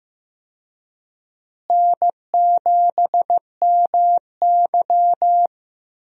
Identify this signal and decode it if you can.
Morse code, heard as a keyed tone, about 15 wpm, 700 Hz. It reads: N7MY
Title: USA Call Signs - 4 Characters - Repeated 2x - 15wpm